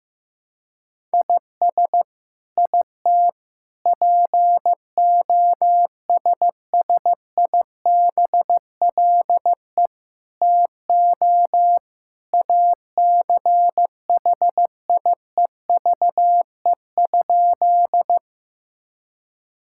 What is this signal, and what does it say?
Morse code, heard as a keyed tone, about 15 wpm, 700 Hz. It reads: IS IT POSSIBLE TO ACHIEVE?
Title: Sentences from Top 1000 Words - Review Part 6 - 15wpm